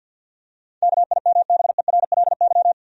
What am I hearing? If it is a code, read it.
Morse code, heard as a keyed tone, about 53 wpm, 700 Hz. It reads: KIMBERLY
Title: USA Common Names - 50wpm